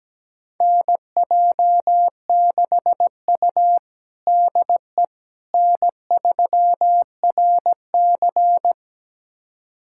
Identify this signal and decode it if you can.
Morse code, heard as a keyed tone, about 17 wpm, 700 Hz. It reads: NJ6U DE N3RC
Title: USA Callsign DE Callsign - 4 Characters - 17wpm